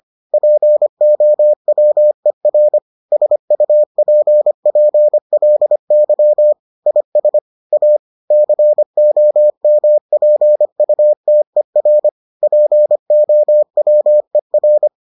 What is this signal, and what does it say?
Morse code, heard as a keyed tone, about 25 wpm, 600 Hz. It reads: POWER SUPPLY IS A COMPUTER POWER